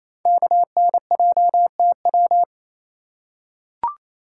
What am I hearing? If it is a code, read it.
Morse code, heard as a keyed tone, about 28 wpm, 700 Hz. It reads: KNJTW
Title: Code Groups - 5 Letter:Numbers - 28wpm